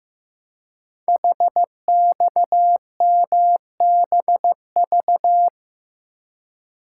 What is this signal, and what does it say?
Morse code, heard as a keyed tone, about 15 wpm, 700 Hz. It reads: HXMBV
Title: Code Groups - 5 Letters - 15wpm